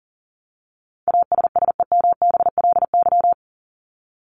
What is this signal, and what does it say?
Morse code, heard as a keyed tone, about 40 wpm, 700 Hz. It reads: ASSEMBLY
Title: LY Suffix - Top 100 Words with Base Words 40wpm